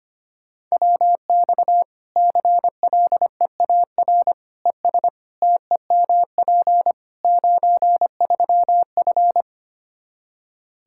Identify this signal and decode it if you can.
Morse code, heard as a keyed tone, about 25 wpm, 700 Hz. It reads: WX CLEAR ES TEMP 93F
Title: K1USN SST QSO Practice 25wpm